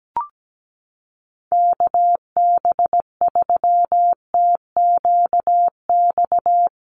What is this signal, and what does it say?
Morse code, heard as a keyed tone, about 17 wpm, 700 Hz. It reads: KB3TQX
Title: USA Call Signs Repeated 2x 17wpm